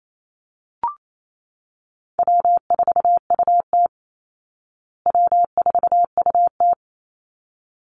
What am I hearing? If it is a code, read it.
Morse code, heard as a keyed tone, about 28 wpm, 700 Hz. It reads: W4UT W4UT
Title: Call Signs Repeated 3x - 28wpm